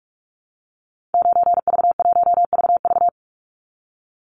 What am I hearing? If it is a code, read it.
Morse code, heard as a keyed tone, about 44 wpm, 700 Hz. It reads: 9V1VV
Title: Call Signs Repeated 2x - 45wpm